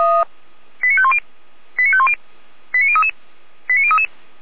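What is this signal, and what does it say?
Five-tone selective call (ZVEI), heard as quick runs of five tones, tones of about 70 ms. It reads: 89510 89510 89033 89033
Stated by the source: Five-tone ZVEI selective call received by SDR at feuerwehr-orlamuende.org (Orlamünde fire brigade).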